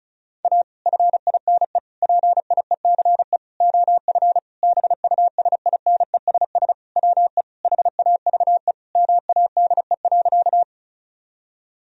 Morse code, heard as a keyed tone, about 36 wpm, 700 Hz. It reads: A FINE PIECE OF BUSINESS WE HAVE MADE.
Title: Sentences from Top 1000 Words - Review Part 2 - 35wpm